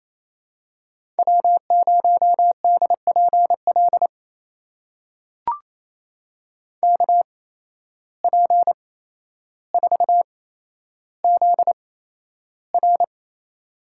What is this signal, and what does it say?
Morse code, heard as a keyed tone, about 28 wpm, spaced as for 12 wpm, 700 Hz. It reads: W0DPL KP4ZR
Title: USA Call Signs - 5 Characters - 8x Character Spacing - 28wpm